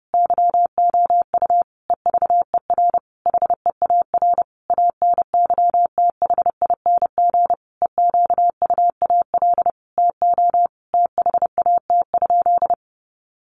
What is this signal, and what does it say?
Morse code, heard as a keyed tone, about 30 wpm, 700 Hz. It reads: YOU EVER HEAR ANYTHING EQUAL TO THAT?
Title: Sentences from Top 1400 Words 30wpm